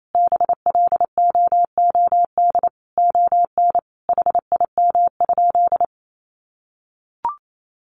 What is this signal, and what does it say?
Morse code, heard as a keyed tone, about 28 wpm, 700 Hz. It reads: BLOOD ON HIM?
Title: Sentences from Top 1100 Words 28wpm